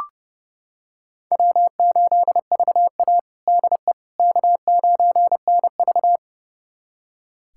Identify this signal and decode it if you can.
Morse code, heard as a keyed tone, about 32 wpm, 700 Hz. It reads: W8VA DE K9NV
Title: USA Callsign DE Callsign - 4 Characters - 30wpm